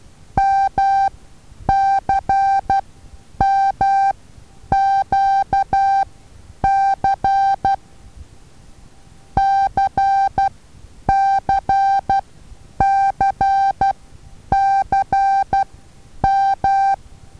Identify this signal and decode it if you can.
Morse code, heard as a keyed tone, about 12 wpm, spaced as for 9 wpm, 780 Hz. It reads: MCMQC CCCCM
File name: Taviro HA6VA A